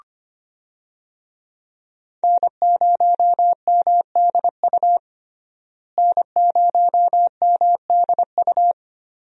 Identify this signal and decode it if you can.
Morse code, heard as a keyed tone, about 25 wpm, 700 Hz. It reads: N0MDU N0MDU
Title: USA Call Signs - 5 Characters - Repeated 3x - 25wpm